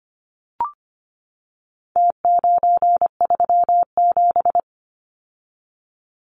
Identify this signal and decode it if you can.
Morse code, heard as a keyed tone, about 25 wpm, 700 Hz. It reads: T937
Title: Numbers - Contest Style - 4-Digit 25wpm